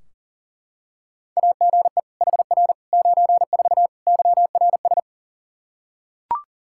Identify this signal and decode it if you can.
Morse code, heard as a keyed tone, about 40 wpm, 700 Hz. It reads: AGE HR 94 YRS
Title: K1USN SST QSO Practice 40wpm